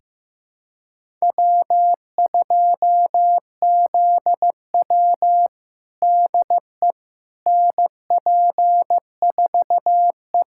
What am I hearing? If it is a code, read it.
Morse code, heard as a keyed tone, about 15 wpm, 700 Hz. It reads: W2ZW DE NP4E
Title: USA Callsign DE Callsign - 4 Characters - 15wpm